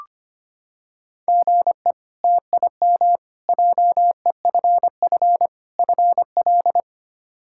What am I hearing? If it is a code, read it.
Morse code, heard as a keyed tone, about 25 wpm, 700 Hz. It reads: GE TIM JEFF FL
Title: K1USN SST QSO Practice - 25wpm